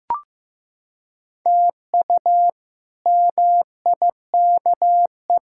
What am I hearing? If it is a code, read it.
Morse code, heard as a keyed tone, about 16 wpm, 700 Hz. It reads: TU MIKE
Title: K1USN SST QSO Practice - 15wpm